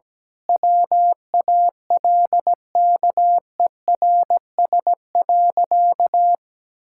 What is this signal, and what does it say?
Morse code, heard as a keyed tone, about 17 wpm, 700 Hz. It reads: WALKERS.
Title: Pangram Challenge v2 - 17wpm